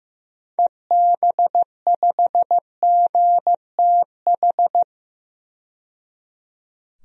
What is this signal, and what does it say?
Morse code, heard as a keyed tone, about 15 wpm, 700 Hz. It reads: EB5GTH
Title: DXCC-100 Call Signs - 6 Characters - 15wpm